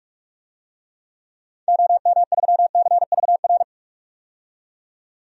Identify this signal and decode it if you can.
Morse code, heard as a keyed tone, about 46 wpm, 700 Hz. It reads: OM3CVR
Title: DXCC-100 Call Signs - 6 Characters - 45wpm